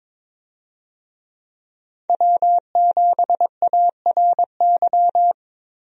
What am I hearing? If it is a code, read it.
Morse code, heard as a keyed tone, about 22 wpm, 700 Hz. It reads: W7ARY
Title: USA Call Signs - 5 Characters - 22wpm